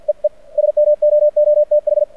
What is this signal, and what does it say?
Morse code, heard as a keyed tone, about 49 wpm, 600 Hz. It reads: EE SMOOTH
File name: archived / CRSnet